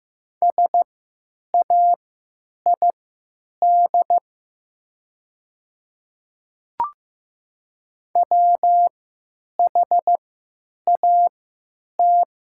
Morse code, heard as a keyed tone, about 15 wpm, spaced as for 9 wpm, 700 Hz. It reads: SAID WHAT
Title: 3x Character Spacing - Top 100 Words - 15wpm